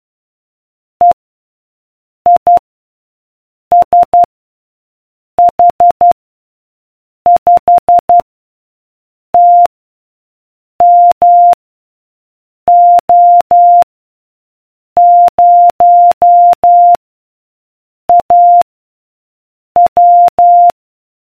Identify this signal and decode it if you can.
Morse code, heard as a keyed tone, about 12 wpm, 700 Hz. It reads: E I S H 5 T M O 0 A W